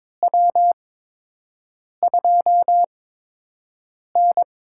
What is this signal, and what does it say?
Morse code, heard as a keyed tone, about 22 wpm, spaced as for 6 wpm, 700 Hz. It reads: W2N
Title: USA Call Signs - 4 Characters - 8x Character Spacing - 22wpm